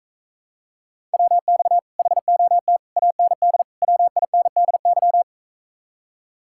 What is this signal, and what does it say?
Morse code, heard as a keyed tone, about 44 wpm, 700 Hz. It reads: WX HOT AND WINDY